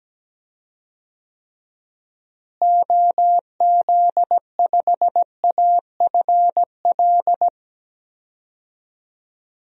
Morse code, heard as a keyed tone, about 17 wpm, 700 Hz. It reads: OZ5AFL
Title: DXCC-100 Call Signs - 6 Characters - Repeated 3x - 17wpm